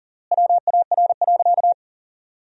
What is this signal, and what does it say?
Morse code, heard as a keyed tone, about 40 wpm, 700 Hz. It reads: WAR.
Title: 3x Word-Spacing - Sentences from Top 400 Words - 40wpm